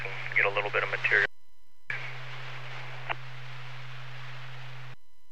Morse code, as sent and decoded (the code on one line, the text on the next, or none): none